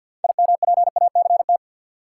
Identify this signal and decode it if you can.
Morse code, heard as a keyed tone, about 52 wpm, 700 Hz. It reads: IMPACT
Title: Sentences from Top 1000 Words - Review Part 6 - New Word Drill - 50wpm